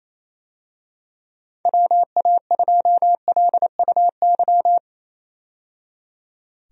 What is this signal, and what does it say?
Morse code, heard as a keyed tone, about 28 wpm, 700 Hz. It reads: WA2LUY
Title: USA Call Signs - 6 Characters - 28wpm